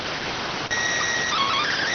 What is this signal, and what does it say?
SSTV (slow-scan television), PD 290.